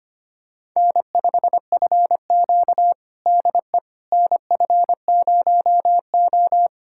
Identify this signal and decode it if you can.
Morse code, heard as a keyed tone, about 25 wpm, 700 Hz. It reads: N5FQ DE NF0O
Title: USA Callsign DE Callsign - 4 Characters - 25wpm